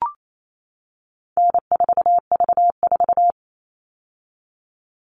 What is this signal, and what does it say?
Morse code, heard as a keyed tone, about 28 wpm, 700 Hz. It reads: N4V4
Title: All Cut Numbers - 4-Digit 28wpm